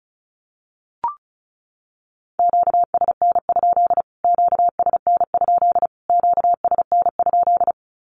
Morse code, heard as a keyed tone, about 34 wpm, 700 Hz. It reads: QSN? QSN? QSN?
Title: Q-Codes Repeated 3x 35wpm